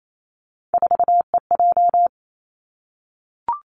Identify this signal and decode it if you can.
Morse code, heard as a keyed tone, about 28 wpm, 700 Hz. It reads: 4EJ
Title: Three Challenging Letter-Number Combinations 28wpm